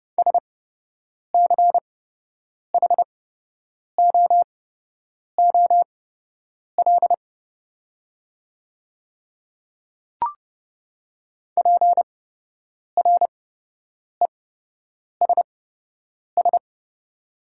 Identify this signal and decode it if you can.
Morse code, heard as a keyed tone, about 30 wpm, spaced as for 8 wpm, 700 Hz. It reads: SCHOOL PRESS
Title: 8x Character Spacing - Top 300 Words - 30wpm